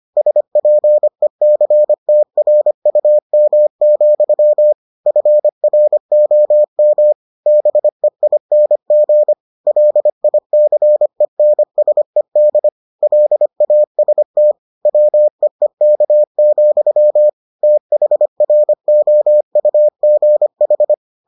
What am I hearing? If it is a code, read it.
Morse code, heard as a keyed tone, about 25 wpm, 600 Hz. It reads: SPECTRUM, FROM BEING LICENSED LAST WEEK, THROUGH